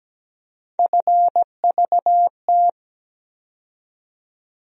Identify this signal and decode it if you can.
Morse code, heard as a keyed tone, about 17 wpm, 700 Hz. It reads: FVT
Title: USA Call Sign Postfixes - 3 Letters 17wpm